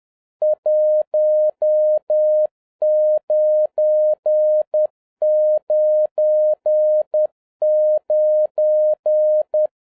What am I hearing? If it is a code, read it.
Morse code, heard as a keyed tone, about 10 wpm, 610 Hz. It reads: 1999